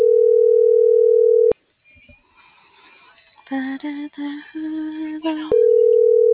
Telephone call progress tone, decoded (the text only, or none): ringback tone